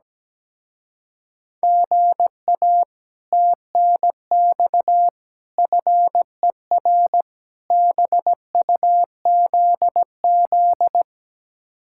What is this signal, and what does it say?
Morse code, heard as a keyed tone, about 17 wpm, 700 Hz. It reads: GA TNX FER BUZZ
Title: Speed-Racing Common QSO Phrases - 17wpm